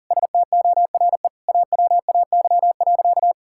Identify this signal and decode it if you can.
Morse code, heard as a keyed tone, about 40 wpm, 700 Hz. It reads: STORE AWAY.
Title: Sentences from Top 1000 Words - Review Part 5 - 40wpm